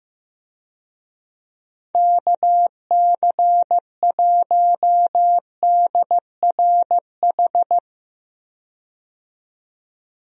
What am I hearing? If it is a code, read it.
Morse code, heard as a keyed tone, about 15 wpm, 700 Hz. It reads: KC1DRH
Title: USA Call Signs - 6 Characters - 15wpm